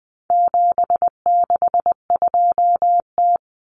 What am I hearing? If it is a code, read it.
Morse code, heard as a keyed tone, about 20 wpm, 700 Hz. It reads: 762T
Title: Common Cut Numbers - 4-Digit 20wpm